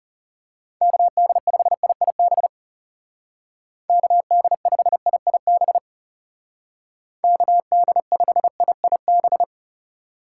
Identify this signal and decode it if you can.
Morse code, heard as a keyed tone, about 36 wpm, 700 Hz. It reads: KD5IIB KD5IIB KD5IIB
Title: Speed Racing - USA 2x3 Call Signs 30wpm